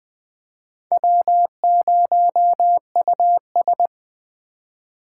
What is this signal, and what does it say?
Morse code, heard as a keyed tone, about 20 wpm, 700 Hz. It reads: W0US
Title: Speed Racing - USA 1x2 Call Signs 17wpm